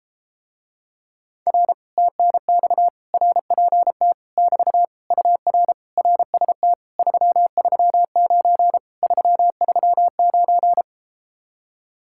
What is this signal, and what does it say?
Morse code, heard as a keyed tone, about 33 wpm, 700 Hz. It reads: R TNX RPT <BT> UR RST 339 339